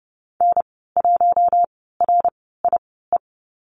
Morse code, heard as a keyed tone, about 30 wpm, spaced as for 17 wpm, 700 Hz. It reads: N1RIE
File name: Lesson 14g - Number 2 - Call Signs 30wpm